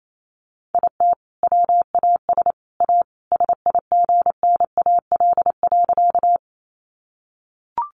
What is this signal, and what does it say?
Morse code, heard as a keyed tone, about 28 wpm, 700 Hz. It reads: IT WAS A SIGNAL.
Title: Sentences from Top 1400 Words 28wpm